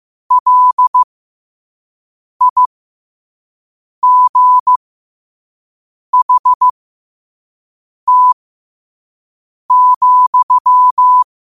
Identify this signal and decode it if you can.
Morse code, heard as a keyed tone, about 15 wpm, spaced as for 5 wpm, 1000 Hz. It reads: LIGHT,
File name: qotdfiles / 5wpm